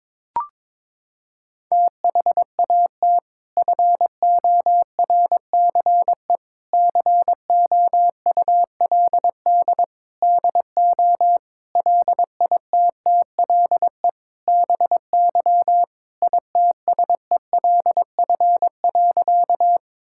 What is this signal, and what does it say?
Morse code, heard as a keyed tone, about 22 wpm, 700 Hz. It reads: THAT FORCE COULD DO LITTLE BY ITSELF.
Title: Sentences from Top 1000 Words - Review Part 5 - 22wpm